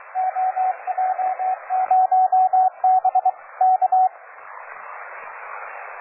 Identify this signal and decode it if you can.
Morse code, heard as a keyed tone, about 23 wpm, 750 Hz, with radio noise. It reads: OJ0B K
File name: cw forum / OJ0B sur 1823 le 21-09-2008 17h00 test GP XXL